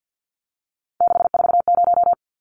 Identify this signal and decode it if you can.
Morse code, heard as a keyed tone, about 50 wpm, 700 Hz. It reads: 640
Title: Numbers - 3-Digit 50wpm